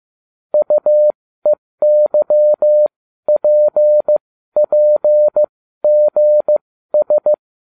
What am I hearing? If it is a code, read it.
Morse code, heard as a keyed tone, about 15 wpm, spaced as for 12 wpm, 600 Hz. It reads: UEYPPGS